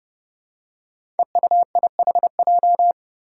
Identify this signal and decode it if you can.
Morse code, heard as a keyed tone, about 32 wpm, 700 Hz. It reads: EUIHJ
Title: Five Challenging Letter-Number Combinations 30wpm